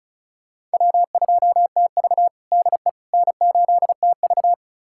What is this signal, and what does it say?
Morse code, heard as a keyed tone, about 36 wpm, 700 Hz. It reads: W2TV DE N8TV
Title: USA Callsign DE Callsign - 4 Characters - 35wpm